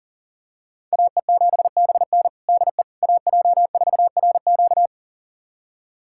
Morse code, heard as a keyed tone, about 40 wpm, 700 Hz. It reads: AE7BN DE AJ4RQ
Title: USA Callsign DE Callsign - 5 Characters - 40wpm